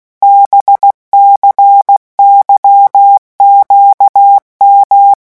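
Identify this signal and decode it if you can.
Morse code, heard as a keyed tone, about 16 wpm, 800 Hz. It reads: BCYQM
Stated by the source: Morse letter-group revision drill at 15 words per minute.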